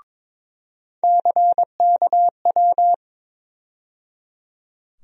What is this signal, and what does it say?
Morse code, heard as a keyed tone, about 22 wpm, 700 Hz. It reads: CKW
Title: USA Call Sign Postfixes - 3 Letters 22wpm